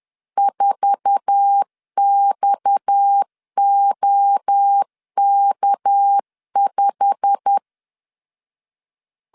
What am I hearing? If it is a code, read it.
Morse code, heard as a keyed tone, about 10 wpm, 800 Hz. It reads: 4XOK5
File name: LEZIONI / 8 YJ45 / 8 YJ45 10wpm 2549